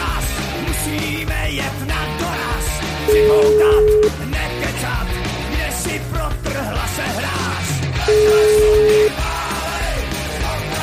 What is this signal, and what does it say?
Ringback tone (Europe, 425 Hz), with speech.